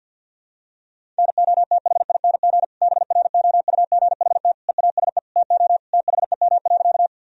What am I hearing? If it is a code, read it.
Morse code, heard as a keyed tone, about 52 wpm, 700 Hz. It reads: NOTHING BROUGHT EASE TO THEM.
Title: Sentences from Top 1000 Words - Review Part 1 - 50wpm